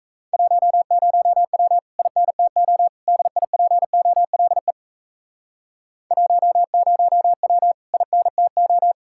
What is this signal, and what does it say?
Morse code, heard as a keyed tone, about 37 wpm, 700 Hz. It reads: 10W INTO DIPOLE 10W INTO